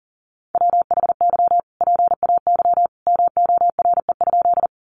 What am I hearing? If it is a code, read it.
Morse code, heard as a keyed tone, about 40 wpm, 700 Hz. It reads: WHY PAY MORE?
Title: Sentences from Top 700 Words 40wpm